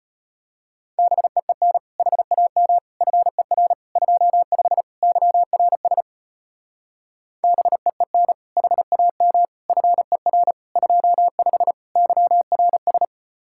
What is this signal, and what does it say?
Morse code, heard as a keyed tone, about 33 wpm, 700 Hz. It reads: BEEN HAM FER 25 YRS BEEN HAM FER 25 YRS